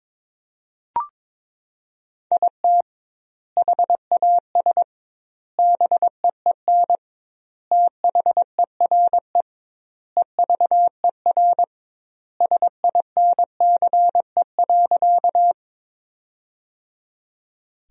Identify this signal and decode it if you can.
Morse code, heard as a keyed tone, about 22 wpm, 700 Hz. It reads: IT HAS BEEN THERE EVER SINCE.
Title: 2x Word-Spacing - Sentences from Top 400 Words - 22wpm